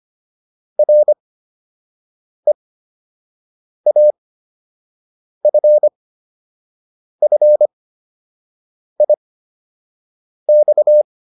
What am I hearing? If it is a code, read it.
Morse code, heard as a keyed tone, about 25 wpm, spaced as for 6 wpm, 600 Hz. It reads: REAFFIX
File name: RE7 06